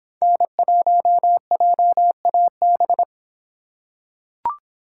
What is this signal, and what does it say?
Morse code, heard as a keyed tone, about 26 wpm, 700 Hz. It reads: N1JAB